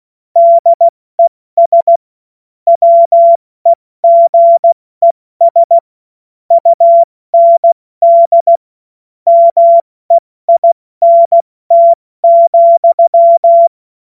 Morse code, heard as a keyed tone, about 16 wpm, 680 Hz. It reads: DES WEGES UND MEINT,
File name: Morsekurs / Witze / Witz 10.Oktober 2023 70 F80